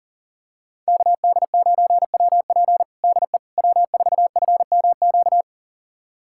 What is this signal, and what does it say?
Morse code, heard as a keyed tone, about 40 wpm, 700 Hz. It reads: KD9WP DE W4FMQ
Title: USA Callsign DE Callsign - 5 Characters - 40wpm